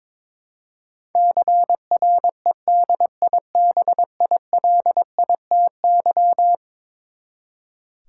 Morse code, heard as a keyed tone, about 21 wpm, 700 Hz. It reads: CREDIBILITY
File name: practice / 22wpm / Eleven Letter Words - 22wpm